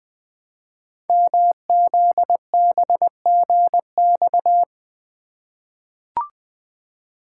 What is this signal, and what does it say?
Morse code, heard as a keyed tone, about 20 wpm, 700 Hz. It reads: MZBGX
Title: Code Groups - 5 Letters - 20wpm